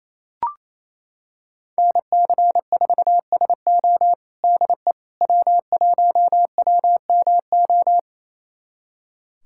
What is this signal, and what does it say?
Morse code, heard as a keyed tone, about 28 wpm, 700 Hz. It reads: NC4SO DE W1WMO
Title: USA Callsign DE Callsign - 5 Characters - 28wpm